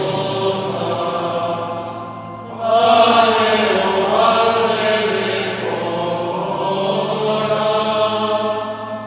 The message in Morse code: none